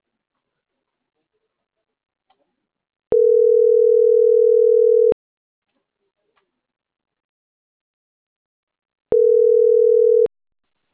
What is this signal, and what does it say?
Ringback tone (North America), 440 + 480 Hz.